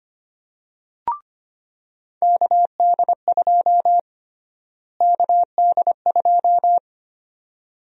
Morse code, heard as a keyed tone, about 25 wpm, 700 Hz. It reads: KD2 KD2
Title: USA Call Sign Prefixes - Repeated 2x - 25wpm